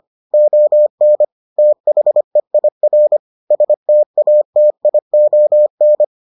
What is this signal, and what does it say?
Morse code, heard as a keyed tone, about 25 wpm, 600 Hz. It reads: ON THEIR STATION